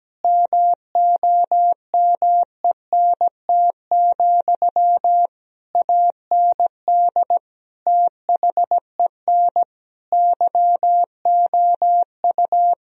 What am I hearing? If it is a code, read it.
Morse code, heard as a keyed tone, about 17 wpm, 700 Hz. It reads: MOMENT, AND THEN YOU
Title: Sentences from Top 1000 Words - Review Part 4 - 17wpm